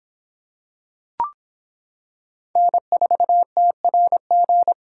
Morse code, heard as a keyed tone, about 25 wpm, 700 Hz. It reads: N4TRG